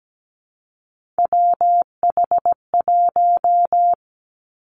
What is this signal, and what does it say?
Morse code, heard as a keyed tone, about 17 wpm, 700 Hz. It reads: WH1
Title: Three Challenging Letter-Number Combinations 17wpm